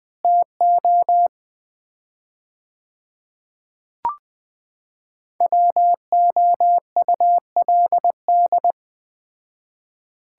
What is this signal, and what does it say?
Morse code, heard as a keyed tone, about 20 wpm, 700 Hz. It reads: TO WOULD
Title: 4x Word-Spacing - Sets of 4 Words - 20wpm